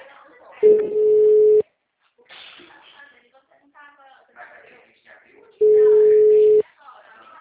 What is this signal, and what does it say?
Ringback tone (Europe, 425 Hz), with speech.
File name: out- -2020- - - .17613